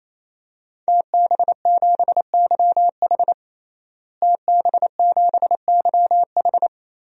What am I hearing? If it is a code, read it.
Morse code, heard as a keyed tone, about 28 wpm, 700 Hz. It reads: TB7YH TB7YH
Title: DXCC-100 Call Signs - 5 Characters - Repeated 3x - 28wpm